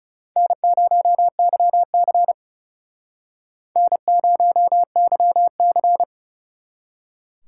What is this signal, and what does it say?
Morse code, heard as a keyed tone, about 37 wpm, 700 Hz. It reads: N0YC N0YC
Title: Speed Racing - USA 1x2 Call Signs 30wpm